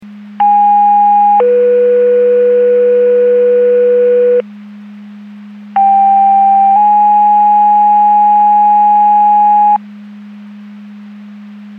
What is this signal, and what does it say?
Two-tone page, heard as a one-second tone, then a three-second tone, about 820 then 490 Hz, about 790 then 830 Hz. It is heard from 0.4 to 9.8 s.